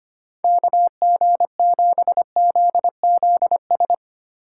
Morse code, heard as a keyed tone, about 32 wpm, 700 Hz. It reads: KG7ZZS